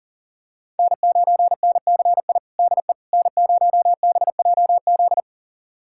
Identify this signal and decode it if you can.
Morse code, heard as a keyed tone, about 37 wpm, 700 Hz. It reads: N9NCI DE N0BJZ
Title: USA Callsign DE Callsign - 5 Characters - 40wpm